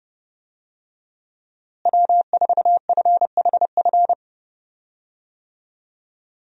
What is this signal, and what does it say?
Morse code, heard as a keyed tone, about 32 wpm, 700 Hz. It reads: W4FHF
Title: USA Call Signs - 5 Characters - Repeated 3x - 30wpm